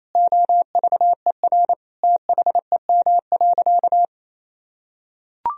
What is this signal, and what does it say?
Morse code, heard as a keyed tone, about 28 wpm, 700 Hz. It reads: OVER THEM.
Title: Sentences from Top 1000 Words - Review Part 3 - 28wpm